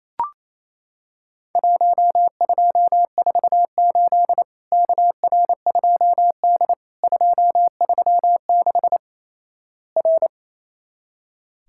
Morse code, heard as a keyed tone, about 28 wpm, 700 Hz. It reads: 1248 KR2D 236 R
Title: Worked All Europe DX Contest - QTC Exchange - 28wpm